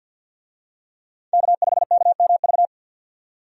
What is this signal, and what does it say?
Morse code, heard as a keyed tone, about 49 wpm, 700 Hz. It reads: K5KMV
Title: USA Call Signs - 5 Characters - 50wpm